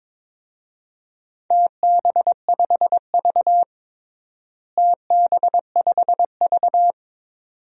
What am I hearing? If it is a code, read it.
Morse code, heard as a keyed tone, about 21 wpm, 700 Hz. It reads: TB5V TB5V
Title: DXCC-100 Call Signs - 4 Characters - Repeated 2x - 22wpm